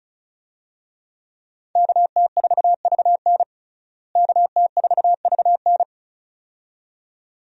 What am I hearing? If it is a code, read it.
Morse code, heard as a keyed tone, about 36 wpm, 700 Hz. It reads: KT4VN KT4VN
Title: USA Call Signs - 5 Characters - Repeated 2x - 35wpm